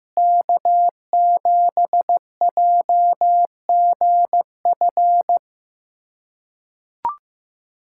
Morse code, heard as a keyed tone, about 16 wpm, 700 Hz. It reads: K7JGF
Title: USA Callsign DE Callsign - 5 Characters - 15wpm